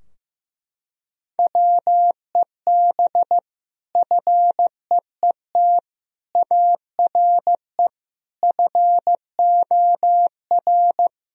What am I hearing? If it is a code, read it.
Morse code, heard as a keyed tone, about 15 wpm, 700 Hz. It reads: WEB FEET ARE FOR
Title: Sentences from Top 1000 Words - Review Part 4 - 15wpm